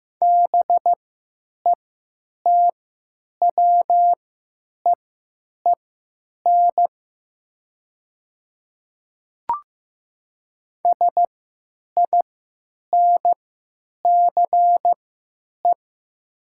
Morse code, heard as a keyed tone, about 15 wpm, spaced as for 8 wpm, 700 Hz. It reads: BETWEEN SINCE
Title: 3x Character Spacing - Top 300 Words - 15wpm